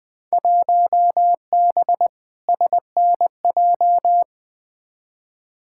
Morse code, heard as a keyed tone, about 20 wpm, 700 Hz. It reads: 1B SNJ
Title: Field Day Exchanges Encore - 20wpm